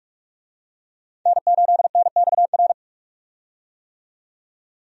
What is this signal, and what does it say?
Morse code, heard as a keyed tone, about 45 wpm, 700 Hz. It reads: N8NXR
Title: USA Call Signs - 5 Characters - 45wpm